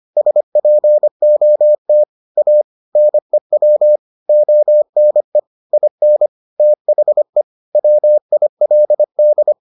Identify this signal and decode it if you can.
Morse code, heard as a keyed tone, about 25 wpm, 600 Hz. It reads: SPOT A NEW ONE IN THE WILD